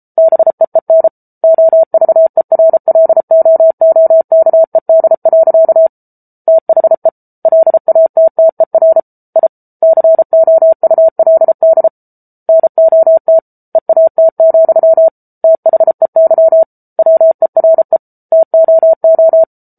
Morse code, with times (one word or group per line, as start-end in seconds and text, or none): 0.2-1.1 BEEN
1.4-5.9 OVERLOOKED.
6.5-7.1 THE
7.5-9.0 LATTER
9.4-9.5 I
9.8-11.9 COULD
12.5-13.4 NOT
13.8-15.1 EAT,
15.4-16.6 THEY
17.0-18.0 WERE
18.3-19.4 TOO